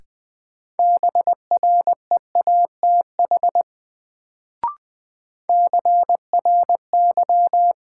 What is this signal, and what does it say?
Morse code, heard as a keyed tone, about 20 wpm, 700 Hz. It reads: BREATH CRY
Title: Sets of 5 Words Encore - New Word Drill - 20wpm